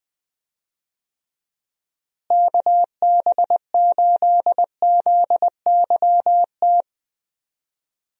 Morse code, heard as a keyed tone, about 20 wpm, 700 Hz. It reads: KB8ZYT
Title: USA Call Signs - 6 Characters - Repeated 3x - 20wpm